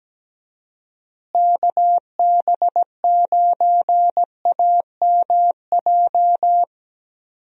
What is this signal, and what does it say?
Morse code, heard as a keyed tone, about 17 wpm, 700 Hz. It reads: KB9AMJ
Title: USA Call Signs - 6 Characters - Repeated 3x - 17wpm